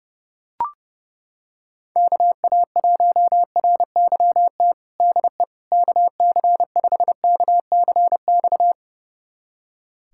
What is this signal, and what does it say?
Morse code, heard as a keyed tone, about 32 wpm, 700 Hz. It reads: KA1RYT DE KC5KCX
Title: USA Callsign DE Callsign - 6 Characters - 30wpm